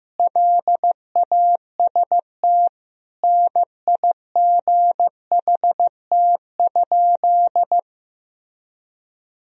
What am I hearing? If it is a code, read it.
Morse code, heard as a keyed tone, about 16 wpm, 700 Hz. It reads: LAST NIGHT?
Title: Sentences from Top 1000 Words - Review Part 4 - 15wpm